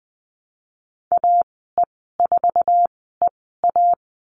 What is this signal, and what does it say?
Morse code, heard as a keyed tone, about 20 wpm, spaced as for 14 wpm, 700 Hz. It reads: AE4EA
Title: 062 Lesson 11g - Letter h - Call Signs 20wpm